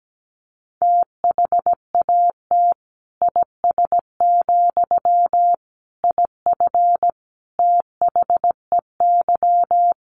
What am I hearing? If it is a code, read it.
Morse code, heard as a keyed tone, about 17 wpm, 700 Hz. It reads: THAT IS, IF THEY
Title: Sentences from Top 1100 Words 17wpm